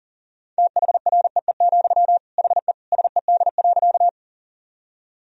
Morse code, heard as a keyed tone, about 40 wpm, 700 Hz. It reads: THREE, HE SED.
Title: Sentences from Top 200 Words and Abbreviations 40wpm